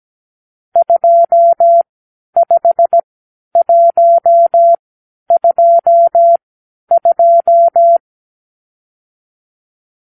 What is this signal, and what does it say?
Morse code, heard as a keyed tone, about 17 wpm, spaced as for 10 wpm, 680 Hz. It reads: 25122
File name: arrl / Review (1-5)